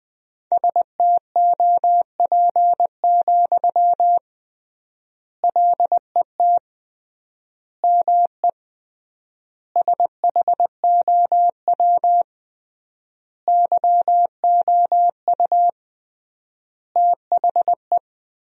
Morse code, heard as a keyed tone, about 20 wpm, 700 Hz. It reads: STOP, LET ME SHOW YOU THE
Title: 3x Word-Spacing - Sentences from Top 300 Words - 20wpm